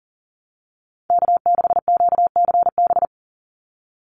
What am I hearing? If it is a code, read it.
Morse code, heard as a keyed tone, about 40 wpm, 700 Hz. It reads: K6QCB
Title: NA SOTA Chaser Callsigns 40wpm